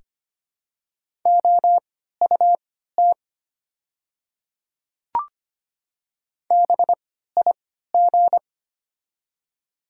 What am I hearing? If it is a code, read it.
Morse code, heard as a keyed tone, about 25 wpm, spaced as for 14 wpm, 700 Hz. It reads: OUT BIG
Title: 3x Character Spacing - Top 200 Words - 25wpm